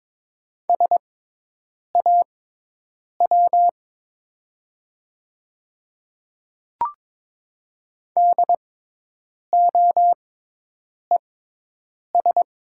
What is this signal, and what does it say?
Morse code, heard as a keyed tone, about 22 wpm, spaced as for 8 wpm, 700 Hz. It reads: SAW DOES
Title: 6x Character Spacing - Top 300 Words - 22wpm